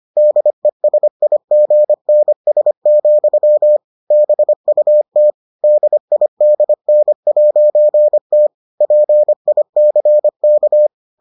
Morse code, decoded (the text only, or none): DESIGNS, BUT DIDN'T PICK